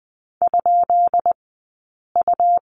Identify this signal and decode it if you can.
Morse code, heard as a keyed tone, about 20 wpm, 700 Hz. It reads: ? U
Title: 132 Lesson 22d - Question Mark - Two Characters 20wpm